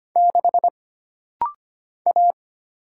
Morse code, heard as a keyed tone, about 25 wpm, 700 Hz. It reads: 6 A
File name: Single Character (Letter:Number:Punctuation:Prosign) - Rapid-Fire - 25wpm